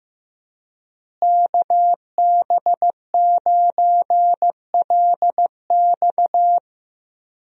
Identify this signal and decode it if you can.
Morse code, heard as a keyed tone, about 15 wpm, 700 Hz. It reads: KB9LX
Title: USA Call Signs - 5 Characters - Repeated 2x - 15wpm